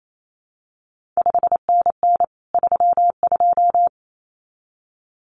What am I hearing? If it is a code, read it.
Morse code, heard as a keyed tone, about 28 wpm, 700 Hz. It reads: 5NN 32
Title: CQ WW DX Contest Exchange - 28wpm